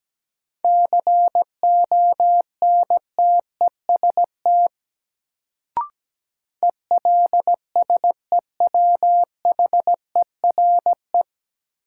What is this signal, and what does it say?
Morse code, heard as a keyed tone, about 17 wpm, 700 Hz. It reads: CONTEST ELSEWHERE
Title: Sentences from Top 1600 Words - New Word Drill Alt - 17wpm